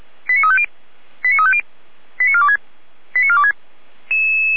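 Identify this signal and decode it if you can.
Five-tone selective call (ZVEI), heard as quick runs of five tones, tones of about 70 ms. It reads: 89350 89350 89426 89426
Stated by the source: Five-tone ZVEI selective call 89108 received on an SDR, from feuerwehr-orlamuende.org fire brigade.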